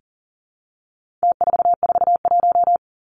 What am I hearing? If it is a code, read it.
Morse code, heard as a keyed tone, about 40 wpm, 700 Hz. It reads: T441
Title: Numbers - Contest Style - 4-Digit 40wpm